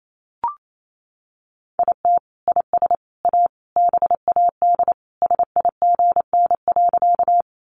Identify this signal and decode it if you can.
Morse code, heard as a keyed tone, about 28 wpm, 700 Hz. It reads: IT IS A BAD SIGN.
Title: Sentences from Top 1000 Words 28wpm